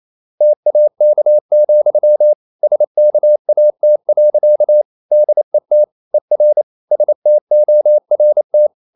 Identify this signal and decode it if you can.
Morse code, heard as a keyed tone, about 28 wpm, 600 Hz. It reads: TAK, SKAT. DET ER STORT
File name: Ferie QRQ 28wpm